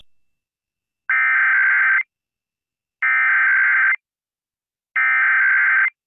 Emergency Alert System (SAME) tones (header bursts), heard as harsh buzzing bursts. It is heard from 1.1 to 5.9 s.